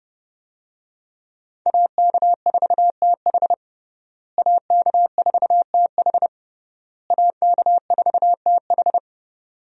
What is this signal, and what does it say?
Morse code, heard as a keyed tone, about 28 wpm, 700 Hz. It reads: AK4TH AK4TH AK4TH
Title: USA Call Signs - 5 Characters - Repeated 3x - 30wpm